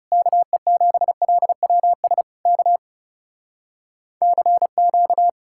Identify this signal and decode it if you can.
Morse code, heard as a keyed tone, about 40 wpm, 700 Hz. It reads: KE7LWS K CQ